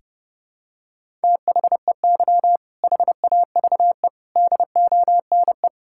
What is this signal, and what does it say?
Morse code, heard as a keyed tone, about 30 wpm, 700 Hz. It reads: THEY HAVE DONE
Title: Sentences from Top 1000 Words - Review Part 5 - 30wpm